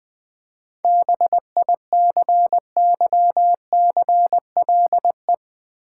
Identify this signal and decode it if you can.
Morse code, heard as a keyed tone, about 20 wpm, 700 Hz. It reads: BICYCLE
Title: Seven Letter Words - 20wpm